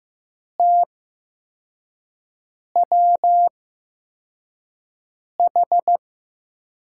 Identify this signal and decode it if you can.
Morse code, heard as a keyed tone, about 15 wpm, spaced as for 4 wpm, 700 Hz. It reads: TWH
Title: USA Call Signs - 8x Character Spacing - 15wpm